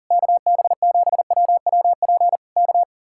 Morse code, heard as a keyed tone, about 35 wpm, 700 Hz. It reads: KB7WWP K
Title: Speed-Racing Common QSO Phrases - 40wpm